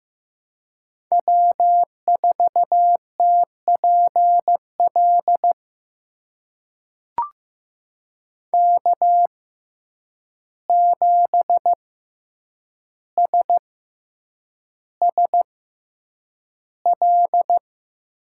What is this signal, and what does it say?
Morse code, heard as a keyed tone, about 15 wpm, spaced as for 8 wpm, 700 Hz. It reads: W4TPL K7SSL
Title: USA Call Signs - 5 Characters - 6x Character Spacing - 15wpm